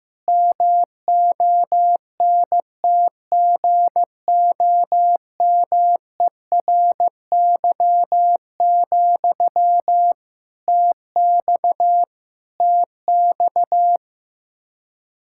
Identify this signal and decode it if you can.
Morse code, heard as a keyed tone, about 15 wpm, 700 Hz. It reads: MONTGOMERY, TX TX
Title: Practice QSO - Exchange 1 - Beginning - 15wpm